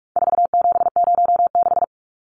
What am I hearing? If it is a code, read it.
Morse code, heard as a keyed tone, about 44 wpm, 700 Hz. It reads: 4706
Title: Numbers - 4-Digit 45wpm